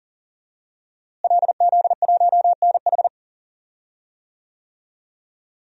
Morse code, heard as a keyed tone, about 40 wpm, 700 Hz. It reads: LZ1NH
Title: DXCC-100 Call Signs - 5 Characters - Repeated 2x - 40wpm